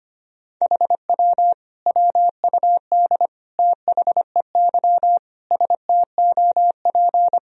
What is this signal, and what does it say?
Morse code, heard as a keyed tone, about 25 wpm, 700 Hz. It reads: HW WUD THEY STOP
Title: Sentences from Top 300 Words and Abbreviations 25wpm